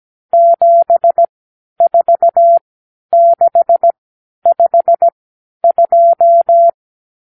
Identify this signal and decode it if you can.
Morse code, heard as a keyed tone, about 17 wpm, spaced as for 10 wpm, 680 Hz. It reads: 74652
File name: k7qo / Lesson 52 (8)